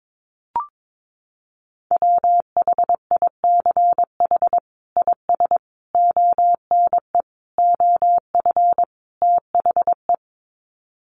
Morse code, heard as a keyed tone, about 22 wpm, 700 Hz. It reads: WHICH IS ONE OF THE
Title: Sets of 5 Words Encore 22wpm